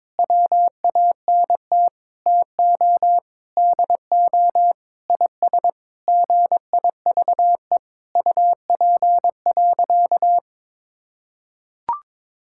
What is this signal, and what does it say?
Morse code, heard as a keyed tone, about 21 wpm, 700 Hz. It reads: WANT TO DO IS GIVE UP.
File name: Sentences with 7-Word Phrases - 22wpm